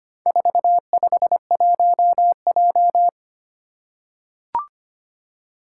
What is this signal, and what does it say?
Morse code, heard as a keyed tone, about 25 wpm, 700 Hz. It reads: 451J
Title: Four Challenging Letter-Number Combinations 25wpm